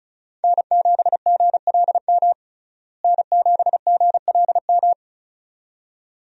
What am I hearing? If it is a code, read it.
Morse code, heard as a keyed tone, about 36 wpm, 700 Hz. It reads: N7GLM N7GLM
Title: USA Call Signs - 5 Characters - Repeated 2x - 35wpm